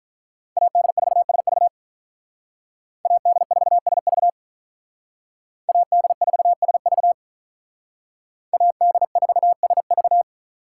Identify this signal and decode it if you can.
Morse code, heard as a keyed tone, about 43 wpm, 700 Hz. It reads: AD4SV AD4SV AD4SV AD4SV
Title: Speed Racing - USA Call Signs 35wpm